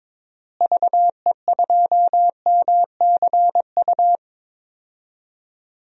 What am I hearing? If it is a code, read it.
Morse code, heard as a keyed tone, about 21 wpm, 700 Hz. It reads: VE2MCU
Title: DXCC-100 Call Signs - 6 Characters - Repeated 3x - 22wpm